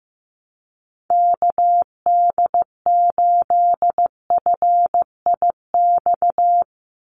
Morse code, heard as a keyed tone, about 15 wpm, 700 Hz. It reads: KD8FIX
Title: USA Call Signs 15wpm